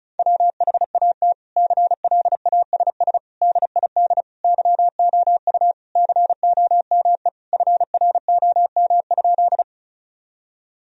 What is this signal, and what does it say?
Morse code, heard as a keyed tone, about 36 wpm, 700 Hz. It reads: WHAT CLASS DID YOU COME FROM?
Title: Sentences from Top 1000 Words - Review Part 2 - 35wpm